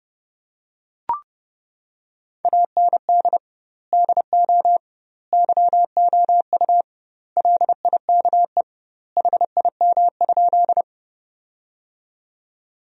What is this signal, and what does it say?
Morse code, heard as a keyed tone, about 32 wpm, 700 Hz. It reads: AND DO YOU LIKE HIM?
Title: 2x Word-Spacing - Sentences from Top 100 Words - 30wpm